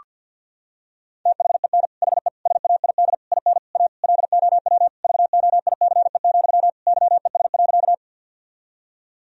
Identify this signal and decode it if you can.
Morse code, heard as a keyed tone, about 52 wpm, 700 Hz. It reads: THEN HE SAID IN A LOW VOICE, YES.
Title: Sentences from Top 1000 Words - Review Part 1 - 50wpm